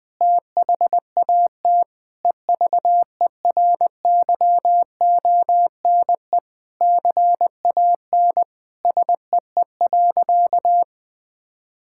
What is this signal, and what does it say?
Morse code, heard as a keyed tone, about 20 wpm, 700 Hz. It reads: THAT EVERYONE CAN SEE.
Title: Sentences with 5-Word Phrases - 20wpm